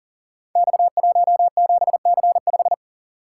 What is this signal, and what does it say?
Morse code, heard as a keyed tone, about 39 wpm, 700 Hz. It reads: X17C5
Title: Code Groups - 5 Letter:Numbers - 40wpm